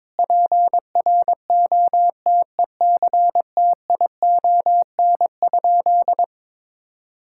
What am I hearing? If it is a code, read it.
Morse code, heard as a keyed tone, about 21 wpm, 700 Hz. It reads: PROTECTION?
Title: Sentences from Top 1600 Words 22wpm